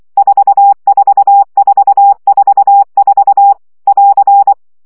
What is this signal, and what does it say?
Morse code, heard as a keyed tone, about 24 wpm, 800 Hz. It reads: 44444 <AR>
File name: CW gyakorlas 4 20 baud